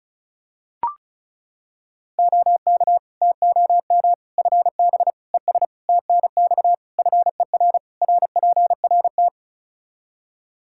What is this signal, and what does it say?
Morse code, heard as a keyed tone, about 36 wpm, 700 Hz. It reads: OK TOM FB ES TNX FER RPRT
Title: Practice QSO - Exchange 2 - Info - 35wpm